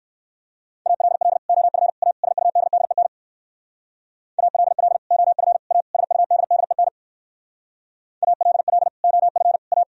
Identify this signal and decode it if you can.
Morse code, heard as a keyed tone, about 58 wpm, 700 Hz. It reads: ALL OF A SUDDEN ALL OF A SUDDEN ALL OF A